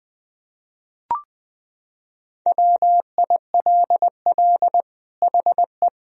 Morse code, heard as a keyed tone, about 20 wpm, 700 Hz. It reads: WILL HE
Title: Sentences from Top 1000 Words - Review Part 3 - 20wpm